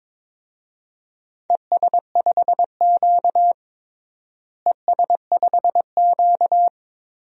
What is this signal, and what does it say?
Morse code, heard as a keyed tone, about 22 wpm, 700 Hz. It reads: ES5Q ES5Q
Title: DXCC-100 Call Signs - 4 Characters - Repeated 2x - 22wpm